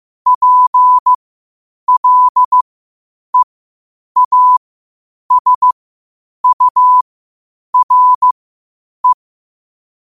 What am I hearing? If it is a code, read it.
Morse code, heard as a keyed tone, about 15 wpm, spaced as for 8 wpm, 1000 Hz. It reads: PLEASURE
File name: qotdfiles / 8wpm